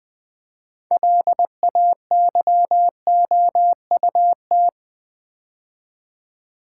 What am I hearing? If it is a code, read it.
Morse code, heard as a keyed tone, about 20 wpm, 690 Hz. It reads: LAYOUT
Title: Build A Word - 2 Letters At A Time - 6-Letter Words - 20wpm